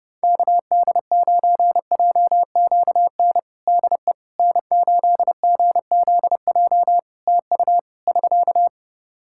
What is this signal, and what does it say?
Morse code, heard as a keyed tone, about 31 wpm, 700 Hz. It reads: KD9JQN DE N8GZJ TU <SK>
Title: K1USN SST QSO Practice 30wpm